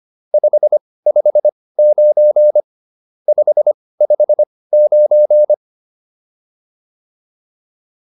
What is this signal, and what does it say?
Morse code, heard as a keyed tone, about 25 wpm, spaced as for 18 wpm, 600 Hz. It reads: 559 559
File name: QSO410 18